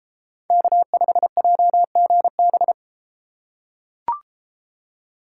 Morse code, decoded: K5JGB

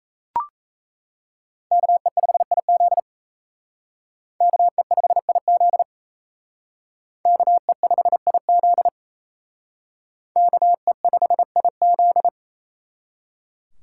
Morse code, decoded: KE5IZ KE5IZ KE5IZ KE5IZ

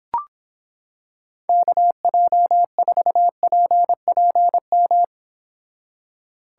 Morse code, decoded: KJ4PPM

